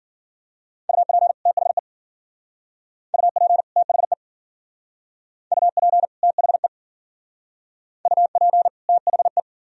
Morse code, heard as a keyed tone, about 50 wpm, 700 Hz. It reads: UP THE UP THE UP THE UP THE
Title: Speed-Racing - Sets of 2 Words 40wpm